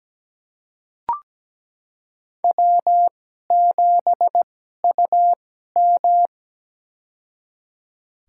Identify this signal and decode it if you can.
Morse code, heard as a keyed tone, about 17 wpm, spaced as for 12 wpm, 700 Hz. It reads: W7UM
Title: USA Call Signs - 4 Characters - 2x Character Spacing - 17wpm